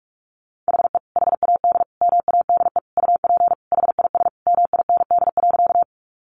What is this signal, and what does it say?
Morse code, heard as a keyed tone, about 44 wpm, 700 Hz. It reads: HE HAD MADE UP HIS MIND.
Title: Sentences from Top 500 Words 45wpm